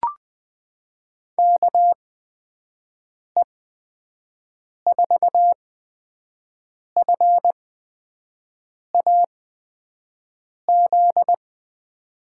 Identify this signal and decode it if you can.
Morse code, heard as a keyed tone, about 21 wpm, spaced as for 5 wpm, 700 Hz. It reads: KE4FAZ
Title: USA Call Signs - 8x Character Spacing - 20wpm